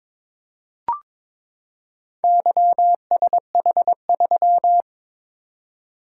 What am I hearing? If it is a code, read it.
Morse code, heard as a keyed tone, about 22 wpm, 700 Hz. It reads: YSH3
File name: practice / 22wpm / Four Letter-Number Combinations - 22wpm